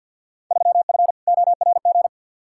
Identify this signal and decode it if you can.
Morse code, heard as a keyed tone, about 48 wpm, 700 Hz. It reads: 3F ORG
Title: Field Day Exchanges - 50wpm